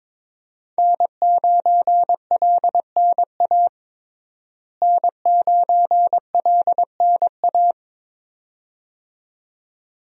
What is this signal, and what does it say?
Morse code, heard as a keyed tone, about 22 wpm, 700 Hz. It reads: N9LNA N9LNA
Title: USA Call Signs - 5 Characters - Repeated 2x - 22wpm